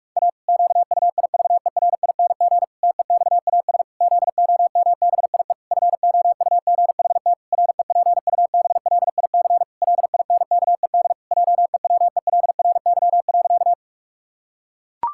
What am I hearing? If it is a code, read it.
Morse code, heard as a keyed tone, about 46 wpm, 700 Hz. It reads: A QUIVERING TEXAS ZOMBIE FOUGHT REPUBLIC LINKED JEWELRY.
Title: Pangram Challenge v2 - 45wpm